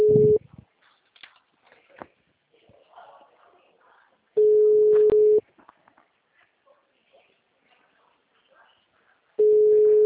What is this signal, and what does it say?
Ringback tone (Europe, 425 Hz).